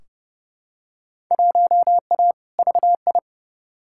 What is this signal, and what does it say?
Morse code, heard as a keyed tone, about 30 wpm, 700 Hz. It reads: 1A VI
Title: Field Day Exchanges Encore - 30wpm